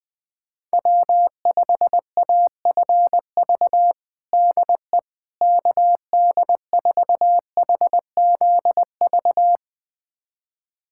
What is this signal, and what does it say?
Morse code, heard as a keyed tone, about 21 wpm, 700 Hz. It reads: W5AFV DE KD4HZV
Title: USA Callsign DE Callsign 20wpm